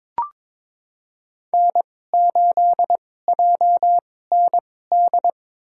Morse code, heard as a keyed tone, about 22 wpm, spaced as for 16 wpm, 700 Hz. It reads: N8JND
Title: USA Call Signs - 2x Character Spacing - 22wpm